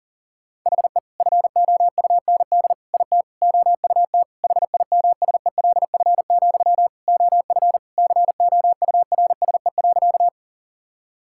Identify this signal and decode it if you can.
Morse code, heard as a keyed tone, about 40 wpm, 700 Hz. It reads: HE FOUND IT OUT HIMSELF, OF COURSE.
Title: Sentences from Top 1600 Words 40wpm